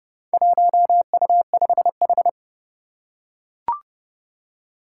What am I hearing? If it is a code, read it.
Morse code, heard as a keyed tone, about 30 wpm, 690 Hz. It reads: 1U5H
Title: Four Challenging Letter-Number Combinations 30wpm